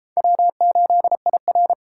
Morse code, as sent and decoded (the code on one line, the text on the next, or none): .-- ---.. .. .-.
W8IR